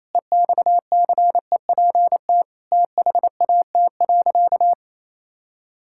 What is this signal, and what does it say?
Morse code, heard as a keyed tone, about 28 wpm, 700 Hz. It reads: EXCEPT THAT.
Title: Sentences from Top 1600 Words 28wpm